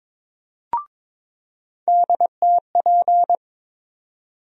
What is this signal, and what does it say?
Morse code, heard as a keyed tone, about 21 wpm, 700 Hz. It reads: DTP
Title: USA Call Sign Postfixes - 3 Letters 22wpm